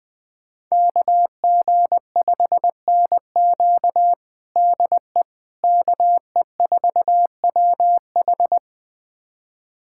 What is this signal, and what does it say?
Morse code, heard as a keyed tone, about 21 wpm, 700 Hz. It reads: KG5NQ DE KE4WH
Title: USA Callsign DE Callsign 20wpm